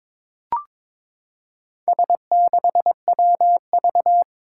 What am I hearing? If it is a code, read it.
Morse code, heard as a keyed tone, about 22 wpm, 700 Hz. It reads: S6WV
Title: Four Challenging Letter-Number Combinations 22wpm